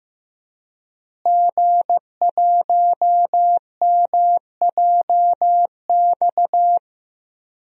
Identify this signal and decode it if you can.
Morse code, heard as a keyed tone, about 15 wpm, 700 Hz. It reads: G1MJX
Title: DXCC-100 Call Signs - 5 Characters - Repeated 3x - 15wpm